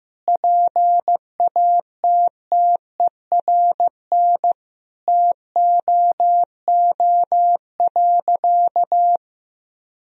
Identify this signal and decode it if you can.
Morse code, heard as a keyed tone, about 16 wpm, 700 Hz. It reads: PATTERN TOO.
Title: Sentences from Top 1000 Words - Review Part 2 - 15wpm